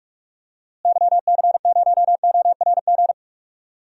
Morse code, heard as a keyed tone, about 46 wpm, 700 Hz. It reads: YC0ORG
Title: DXCC-100 Call Signs - 6 Characters - Repeated 2x - 45wpm